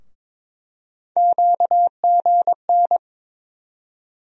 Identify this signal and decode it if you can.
Morse code, heard as a keyed tone, about 22 wpm, 700 Hz. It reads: QGN
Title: USA Call Sign Postfixes - 3 Letters 22wpm